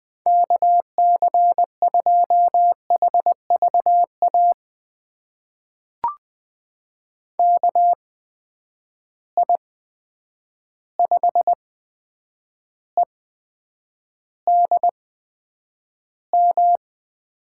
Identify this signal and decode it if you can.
Morse code, heard as a keyed tone, about 21 wpm, spaced as for 9 wpm, 700 Hz. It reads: KC2HVA KI5EDM
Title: USA Call Signs - 6 Characters - 8x Character Spacing - 20wpm